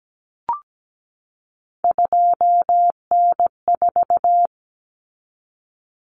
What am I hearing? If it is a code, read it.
Morse code, heard as a keyed tone, about 17 wpm, 700 Hz. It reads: 2N4
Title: Common Cut Numbers - 3-Digit 17wpm